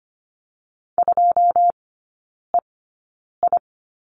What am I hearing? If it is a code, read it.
Morse code, heard as a keyed tone, about 25 wpm, spaced as for 9 wpm, 690 Hz. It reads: 2EI
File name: Lesson 14e - Number 2 - Three Characters 25wpm